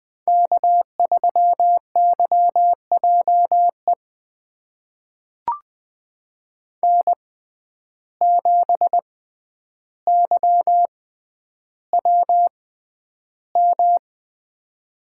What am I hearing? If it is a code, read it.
Morse code, heard as a keyed tone, about 20 wpm, spaced as for 10 wpm, 700 Hz. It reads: K3YJE N7YWM
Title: USA Call Signs - 5 Characters - 6x Character Spacing - 20wpm